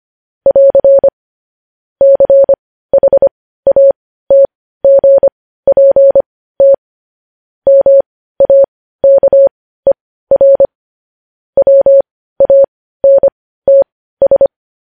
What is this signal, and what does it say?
Morse code, heard as a keyed tone, about 25 wpm, spaced as for 15 wpm, 550 Hz. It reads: <AR> CHATGPT MAKER WANTS
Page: News Headlines in Morse Code at 15 WPM